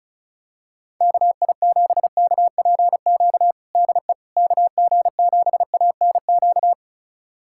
Morse code, heard as a keyed tone, about 34 wpm, 700 Hz. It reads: KI7KPQ DE KG7ANQ
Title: USA Callsign DE Callsign - 6 Characters - 35wpm